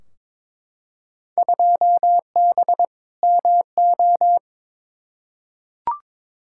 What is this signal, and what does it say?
Morse code, heard as a keyed tone, about 22 wpm, 700 Hz. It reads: 2B MO
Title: Field Day Exchanges Encore - 22wpm